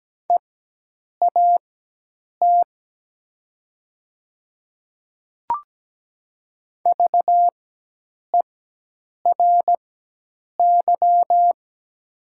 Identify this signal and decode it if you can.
Morse code, heard as a keyed tone, about 17 wpm, spaced as for 8 wpm, 700 Hz. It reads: EAT VERY
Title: 4x Character Spacing - Top 300 Words - 17wpm